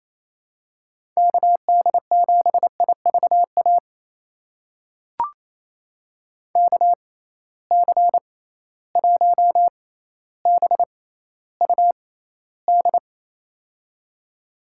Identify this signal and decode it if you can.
Morse code, heard as a keyed tone, about 28 wpm, spaced as for 14 wpm, 700 Hz. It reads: KD7IVA KC1BUD
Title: USA Call Signs - 6 Characters - 6x Character Spacing - 28wpm